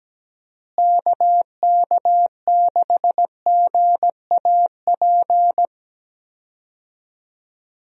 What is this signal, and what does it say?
Morse code, heard as a keyed tone, about 17 wpm, 700 Hz. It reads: KK6GAP
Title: USA Call Signs - 6 Characters - Repeated 3x - 17wpm